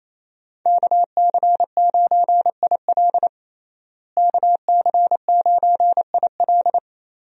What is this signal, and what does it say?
Morse code, heard as a keyed tone, about 28 wpm, 700 Hz. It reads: KC9IL KC9IL
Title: USA Call Signs - 5 Characters - Repeated 2x - 28wpm